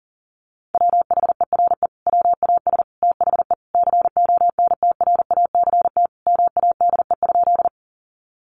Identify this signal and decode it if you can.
Morse code, heard as a keyed tone, about 40 wpm, 700 Hz. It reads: WHERE WAS THE CONTRACT MADE?
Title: Sentences from Top 1400 Words 40wpm